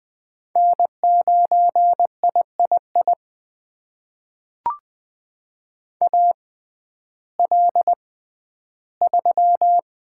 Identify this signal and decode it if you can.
Morse code, heard as a keyed tone, about 20 wpm, spaced as for 12 wpm, 700 Hz. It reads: N9III AL3
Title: USA Call Signs - 6x Character Spacing - 20wpm